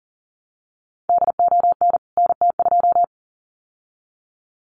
Morse code, heard as a keyed tone, about 37 wpm, 700 Hz. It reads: DON NT2
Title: CWT Exchange 40wpm